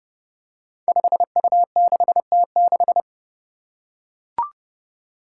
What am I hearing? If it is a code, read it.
Morse code, heard as a keyed tone, about 31 wpm, 700 Hz. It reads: 5U6T6
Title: Five Challenging Letter-Number Combinations 30wpm